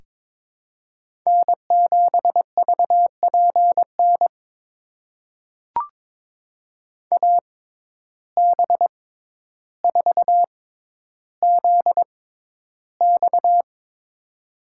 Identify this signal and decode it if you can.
Morse code, heard as a keyed tone, about 22 wpm, spaced as for 11 wpm, 700 Hz. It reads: N7VPN AB4ZX
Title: USA Call Signs - 5 Characters - 6x Character Spacing - 22wpm